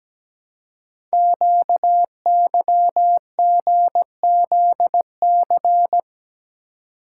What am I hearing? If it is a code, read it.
Morse code, heard as a keyed tone, about 17 wpm, 700 Hz. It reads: QYGZC
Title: Code Groups - 5 Letters - 17wpm